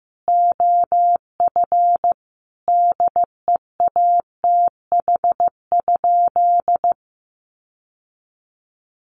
Morse code, heard as a keyed tone, about 15 wpm, 700 Hz. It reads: OF DEATH?
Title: Sentences from Top 1100 Words 15wpm